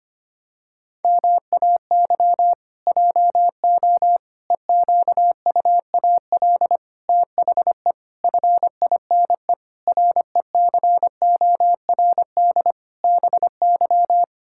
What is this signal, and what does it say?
Morse code, heard as a keyed tone, about 25 wpm, 700 Hz. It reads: MAY JO EQUAL THE FINE RECORD BY
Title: Pangram Challenge v2 - 25wpm